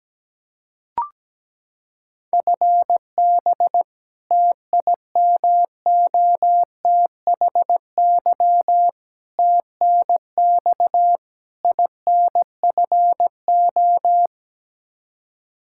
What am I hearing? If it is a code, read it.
Morse code, heard as a keyed tone, about 17 wpm, 700 Hz. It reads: FB TIMOTHY TNX INFO
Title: K1USN SST QSO Practice 17wpm